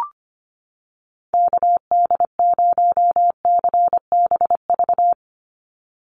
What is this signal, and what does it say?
Morse code, heard as a keyed tone, about 25 wpm, 700 Hz. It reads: KD0CBV
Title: USA Call Signs 25wpm